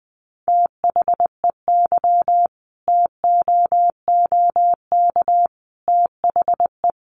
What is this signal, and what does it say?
Morse code, heard as a keyed tone, about 19 wpm, 700 Hz. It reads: THEY TOOK THE
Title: Sentences from Top 1200 Words 20wpm